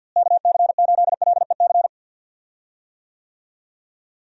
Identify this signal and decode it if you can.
Morse code, heard as a keyed tone, about 50 wpm, 700 Hz. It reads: KC8LEC
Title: USA Call Signs - 6 Characters - Repeated 3x - 50wpm